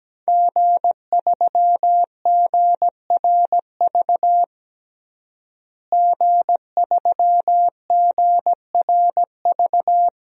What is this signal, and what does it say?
Morse code, heard as a keyed tone, about 17 wpm, 700 Hz. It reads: G3GRV G3GRV
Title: DXCC-100 Call Signs - 5 Characters - Repeated 3x - 17wpm